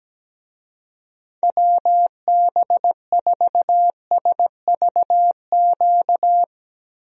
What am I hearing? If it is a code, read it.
Morse code, heard as a keyed tone, about 17 wpm, 700 Hz. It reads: WB4SVQ
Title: USA Call Signs - 6 Characters - Repeated 3x - 17wpm